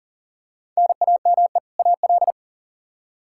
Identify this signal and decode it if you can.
Morse code, heard as a keyed tone, about 37 wpm, 700 Hz. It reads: NAME AL
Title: Practice QSO - Exchange 1 - Beginning - 40wpm